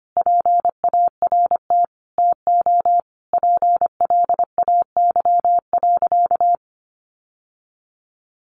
Morse code, heard as a keyed tone, about 25 wpm, 700 Hz. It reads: PART TO PLAY.
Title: Sentences from Top 900 Words 25wpm